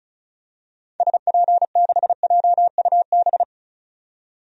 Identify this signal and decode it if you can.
Morse code, heard as a keyed tone, about 36 wpm, 700 Hz. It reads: SP6JUB
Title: DXCC-100 Call Signs - 6 Characters - 35wpm